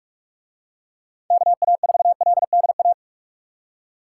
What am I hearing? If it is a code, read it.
Morse code, heard as a keyed tone, about 42 wpm, 700 Hz. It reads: KA4LDA